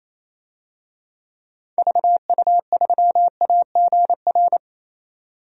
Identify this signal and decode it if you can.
Morse code, heard as a keyed tone, about 28 wpm, 700 Hz. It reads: VU3AGR
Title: DXCC-100 Call Signs - 6 Characters - 28wpm